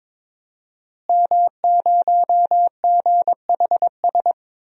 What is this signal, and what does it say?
Morse code, heard as a keyed tone, about 21 wpm, 700 Hz. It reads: M0GHS
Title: DXCC-100 Call Signs - 5 Characters - Repeated 2x - 22wpm